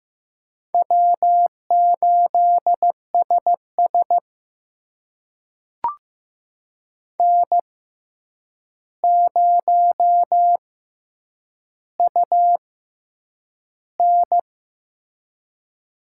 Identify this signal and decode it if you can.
Morse code, heard as a keyed tone, about 15 wpm, spaced as for 8 wpm, 700 Hz. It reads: W8SS N0UN
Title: USA Call Signs - 4 Characters - 6x Character Spacing - 15wpm